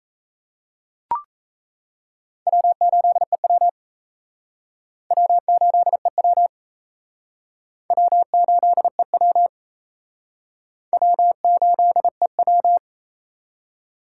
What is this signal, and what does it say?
Morse code, heard as a keyed tone, about 34 wpm, 700 Hz. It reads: W8EW W8EW W8EW W8EW
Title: Speed Racing - USA 1x2 Call Signs 28wpm